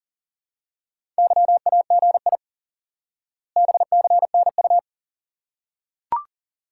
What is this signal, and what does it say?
Morse code, heard as a keyed tone, about 40 wpm, 700 Hz. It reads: YAGI BCNU
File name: IQE Course 3 - Lesson 89 - YAGI - Two Elements - 40wpm